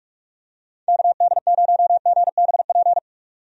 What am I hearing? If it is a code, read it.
Morse code, heard as a keyed tone, about 42 wpm, 700 Hz. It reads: KD0GBP
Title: USA Call Signs - 6 Characters - 45wpm